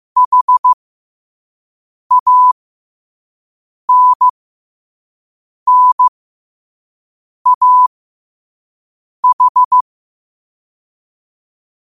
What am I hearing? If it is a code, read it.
Morse code, heard as a keyed tone, about 15 wpm, spaced as for 5 wpm, 1000 Hz. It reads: HANNAH
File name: qotdfiles / 5wpm